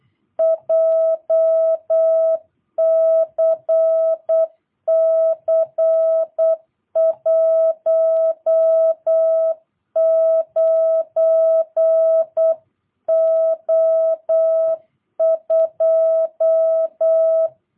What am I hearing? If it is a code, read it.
Morse code, heard as a keyed tone, about 8 wpm, 650 Hz. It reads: JCC 19O2
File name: jh8cbh / 2022cw